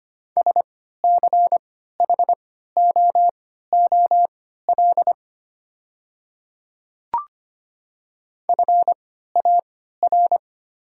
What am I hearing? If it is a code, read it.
Morse code, heard as a keyed tone, about 25 wpm, spaced as for 14 wpm, 690 Hz. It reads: SCHOOL FAR
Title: 3x Character Spacing - Top 300 Words - 25wpm